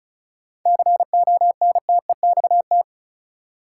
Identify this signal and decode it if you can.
Morse code, heard as a keyed tone, about 36 wpm, 700 Hz. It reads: CONTEXT
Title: Sets of 4 Words - New Word Drill - 35wpm